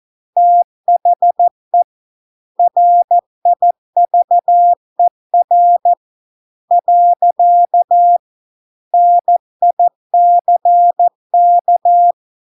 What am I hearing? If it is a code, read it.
Morse code, heard as a keyed tone, about 14 wpm, 700 Hz. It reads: THE RIVER . NICK